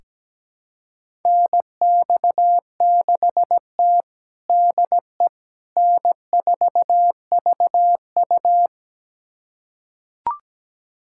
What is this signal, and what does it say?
Morse code, heard as a keyed tone, about 17 wpm, 700 Hz. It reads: NX6T DE N4VU
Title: USA Callsign DE Callsign - 4 Characters - 17wpm